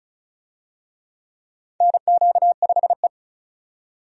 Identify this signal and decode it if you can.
Morse code, heard as a keyed tone, about 36 wpm, 700 Hz. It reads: NQ5E
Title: USA Call Signs - 4 Characters - 35wpm